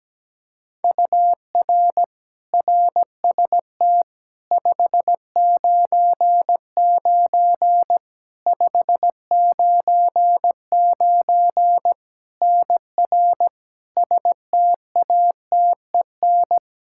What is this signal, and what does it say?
Morse code, heard as a keyed tone, about 17 wpm, 700 Hz. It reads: UR RST 599 599 NR STATEN
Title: Practice QSO - Exchange 1 - Beginning - 17wpm